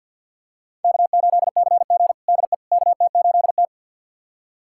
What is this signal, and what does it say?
Morse code, heard as a keyed tone, about 51 wpm, 700 Hz. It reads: K8CG DE KT8T
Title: USA Callsign DE Callsign - 4 Characters - 50wpm